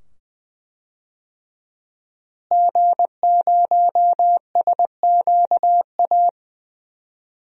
Morse code, heard as a keyed tone, about 20 wpm, 700 Hz. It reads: G0SQA
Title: DXCC-100 Call Signs - 5 Characters - Repeated 2x - 20wpm